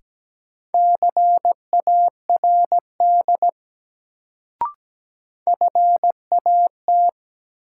Morse code, heard as a keyed tone, about 17 wpm, 700 Hz. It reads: CARD FAT
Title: Sentences from Top 600 Words - New Word Drill - 17wpm